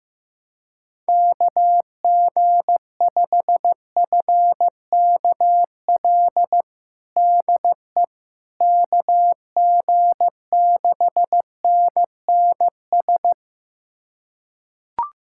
Morse code, heard as a keyed tone, about 15 wpm, 700 Hz. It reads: KG5FKL DE KG6NNS
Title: USA Callsign DE Callsign 15wpm